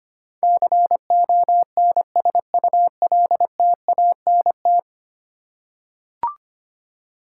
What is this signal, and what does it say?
Morse code, heard as a keyed tone, about 25 wpm, 700 Hz. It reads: CONSULTANT
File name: 10 Letter Words - First 3 Letters Given - 25wpm